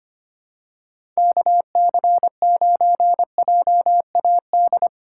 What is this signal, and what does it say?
Morse code, heard as a keyed tone, about 25 wpm, 700 Hz. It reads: KC9JAD
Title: USA Call Signs - 6 Characters - 4x Character Spacing - 25wpm